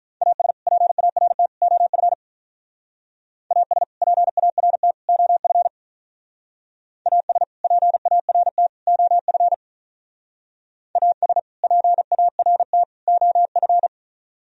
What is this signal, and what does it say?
Morse code, heard as a keyed tone, about 44 wpm, 700 Hz. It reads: AS PART OF AS PART OF AS PART OF AS PART OF